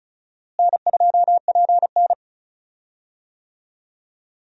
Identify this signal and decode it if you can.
Morse code, heard as a keyed tone, about 34 wpm, 700 Hz. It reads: N2PN
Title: USA Call Signs - 4 Characters - Repeated 3x - 35wpm